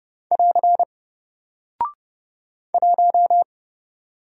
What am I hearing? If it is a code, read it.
Morse code, heard as a keyed tone, about 29 wpm, 700 Hz. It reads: <AR> 1
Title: Single Character (Letter:Number:Punctuation:Prosign) 30wpm